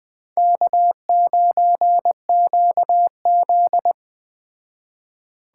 Morse code, heard as a keyed tone, about 20 wpm, 700 Hz. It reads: K9QZ